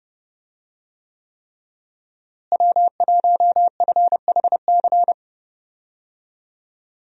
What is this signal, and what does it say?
Morse code, heard as a keyed tone, about 32 wpm, 700 Hz. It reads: W1FHC
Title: USA Call Signs - 5 Characters - Repeated 3x - 30wpm